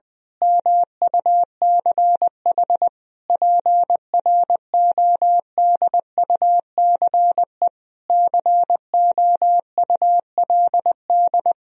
Morse code, heard as a keyed tone, about 20 wpm, 700 Hz. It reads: MUCH PRODUCE COULD
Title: Sentences from Top 1000 Words - Review Part 3 - 20wpm